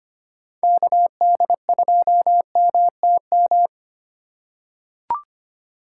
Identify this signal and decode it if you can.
Morse code, heard as a keyed tone, about 25 wpm, 700 Hz. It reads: KD2MTM
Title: USA Call Signs - 6 Characters - 4x Character Spacing - 25wpm